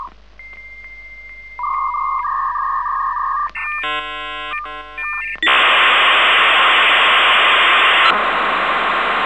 Dial-up modem handshake.